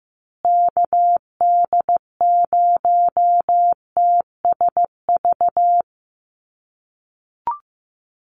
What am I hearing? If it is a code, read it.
Morse code, heard as a keyed tone, about 15 wpm, 700 Hz. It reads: KD0TSV
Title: USA Call Signs Repeated 2x 15wpm